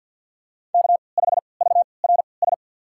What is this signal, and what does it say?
Morse code, heard as a keyed tone, about 50 wpm, spaced as for 28 wpm, 690 Hz. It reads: K5VRS
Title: USA Call Signs - 5 Characters - 3x Character Spacing - 50wpm